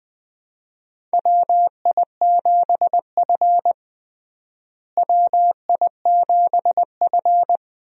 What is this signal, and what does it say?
Morse code, heard as a keyed tone, about 20 wpm, 700 Hz. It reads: WI7F WI7F
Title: USA Call Signs - 4 Characters - Repeated 3x - 20wpm